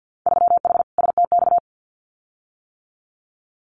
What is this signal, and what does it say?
Morse code, heard as a keyed tone, about 50 wpm, 700 Hz. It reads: 3H STX
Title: Winter Field Day Exchanges Encore 50wpm